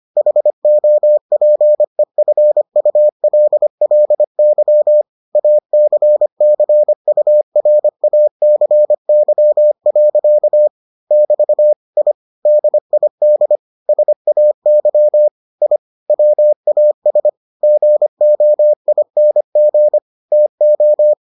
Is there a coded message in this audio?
Yes: Morse code from 0.2 to 21.1 s, heard as a keyed tone, about 25 wpm, 600 Hz. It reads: HOPEFULLY ACCURACY. <BT> I DID SAY I WAS GOING TO